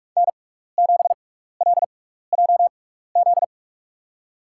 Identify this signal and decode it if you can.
Morse code, heard as a keyed tone, about 46 wpm, spaced as for 16 wpm, 700 Hz. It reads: N7LJZ
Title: USA Call Signs - 6x Character Spacing - 45wpm